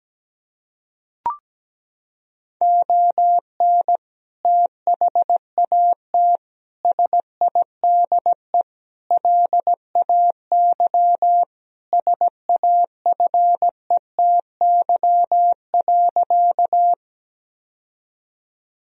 Morse code, heard as a keyed tone, about 17 wpm, 700 Hz. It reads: ON THAT SIDE LAY SAFETY.
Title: Sentences from Top 1000 Words - Review Part 4 - 17wpm